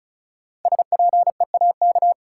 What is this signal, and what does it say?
Morse code, heard as a keyed tone, about 34 wpm, 700 Hz. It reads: SPEAK
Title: Sentences from Top 1000 Words - Review Part 2 - New Word Drill - 35wpm